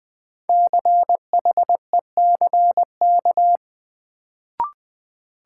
Morse code, heard as a keyed tone, about 20 wpm, 700 Hz. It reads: CHECK
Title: Sentences from Top 1000 Words - Review Part 3 - New Word Drill - 20wpm